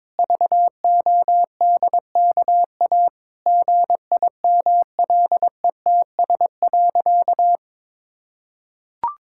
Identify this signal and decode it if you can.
Morse code, heard as a keyed tone, about 22 wpm, 700 Hz. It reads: VODKA GIMLETS.
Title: Pangram Challenge v2 - 22wpm